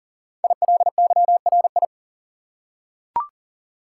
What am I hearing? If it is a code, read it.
Morse code, heard as a keyed tone, about 39 wpm, 690 Hz. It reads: ILYRI
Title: Code Groups - 5 Letters - 40wpm